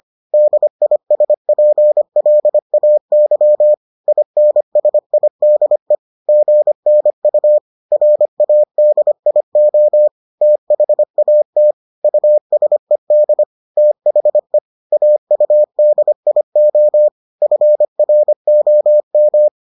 Morse code, heard as a keyed tone, about 25 wpm, 600 Hz. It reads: DISPLAY INSIDE GNU RADIO THAT USED THE AUDIO FROM